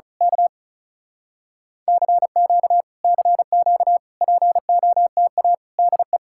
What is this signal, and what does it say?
Morse code, heard as a keyed tone, about 32 wpm, 700 Hz. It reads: K CQ CQ POTA DE
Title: Speed-Racing Common QSO Phrases - 35wpm